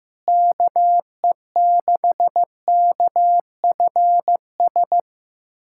Morse code, heard as a keyed tone, about 15 wpm, 700 Hz. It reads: KE6KFS
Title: USA Callsign DE Callsign - 6 Characters - 15wpm